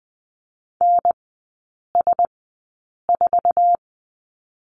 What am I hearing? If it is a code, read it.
Morse code, heard as a keyed tone, about 20 wpm, spaced as for 8 wpm, 700 Hz. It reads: NS4
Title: 060 Lesson 11e - Letter h - Three Characters 20wpm